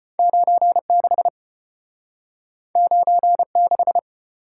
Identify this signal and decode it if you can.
Morse code, heard as a keyed tone, about 33 wpm, 700 Hz. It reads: 9 6 9 6